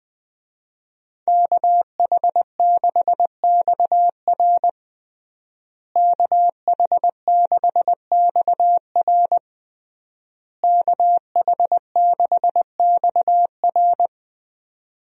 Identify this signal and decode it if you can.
Morse code, heard as a keyed tone, about 20 wpm, 700 Hz. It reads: KH6XR KH6XR KH6XR
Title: DXCC-100 Call Signs - 5 Characters - Repeated 3x - 20wpm